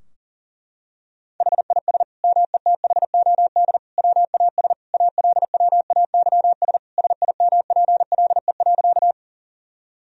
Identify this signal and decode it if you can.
Morse code, heard as a keyed tone, about 40 wpm, 700 Hz. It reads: HIS METHOD WAS ALWAYS SIMPLE.
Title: Sentences from Top 1000 Words - Review Part 3 - 40wpm